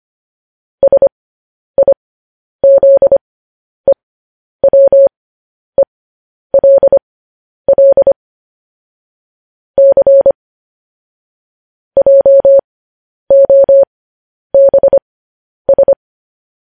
Morse code, heard as a keyed tone, about 25 wpm, spaced as for 10 wpm, 550 Hz. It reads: SIZEWELL C JOBS
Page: news-headlines-in-morse-code-at-10-wpm